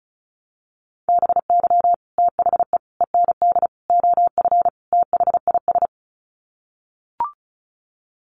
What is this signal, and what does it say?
Morse code, heard as a keyed tone, about 34 wpm, 700 Hz. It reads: BY THE END OF THIS
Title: Sets of 5 Words Encore 35wpm